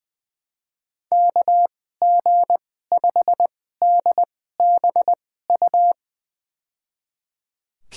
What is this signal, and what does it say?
Morse code, heard as a keyed tone, about 20 wpm, spaced as for 14 wpm, 700 Hz. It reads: KG5DBU
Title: USA Call Signs - 6 Characters - 2x Character Spacing - 20wpm